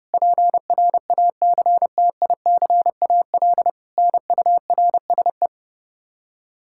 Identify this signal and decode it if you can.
Morse code, heard as a keyed tone, about 29 wpm, 700 Hz. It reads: PRACTICAL NURSE
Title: Practice QSO - Exchange 3 - More Info - 30wpm